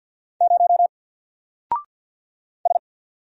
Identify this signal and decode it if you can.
Morse code, heard as a keyed tone, about 46 wpm, 700 Hz. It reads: 0 S
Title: Single Letter-Number 50wpm